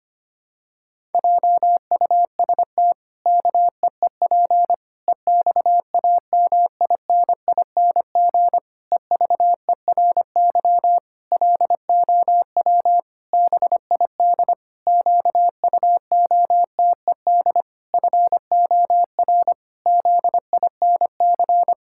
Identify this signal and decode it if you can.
Morse code, heard as a keyed tone, about 25 wpm, 700 Hz. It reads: JUST KEEP EXAMINING EVERY LOW BID QUOTED FOR ZINC
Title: Pangram Challenge v2 - 25wpm